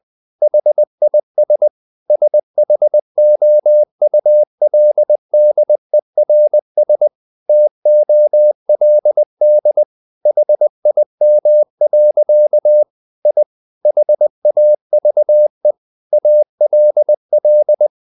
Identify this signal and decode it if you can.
Morse code, heard as a keyed tone, about 20 wpm, 600 Hz. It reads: HIS SHOULDERS TOLD HIM. I HAVE ALL
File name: the old man and the sea part 2 20wpm